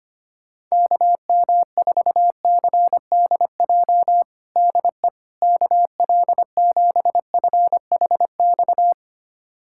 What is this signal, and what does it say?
Morse code, heard as a keyed tone, about 25 wpm, 700 Hz. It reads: KM4CDJ DE KL7FHX
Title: USA Callsign DE Callsign - 6 Characters - 25wpm